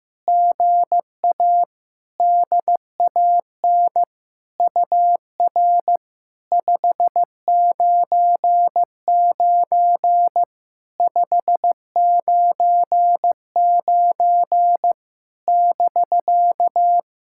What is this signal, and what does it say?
Morse code, heard as a keyed tone, about 15 wpm, 700 Hz. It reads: GA DAN UR 599 599 <BK>
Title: SOTA QSO Practice 15wpm